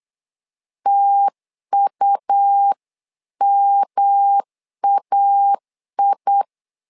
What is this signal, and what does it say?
Morse code, heard as a keyed tone, about 8 wpm, 800 Hz. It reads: TUMAI
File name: LEZIONI / 3 MWRO / 3 MW 8wpm 2501